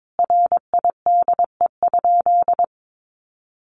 Morse code, heard as a keyed tone, about 22 wpm, 700 Hz. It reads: RIDE?
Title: Sentences from Top 1200 Words 22wpm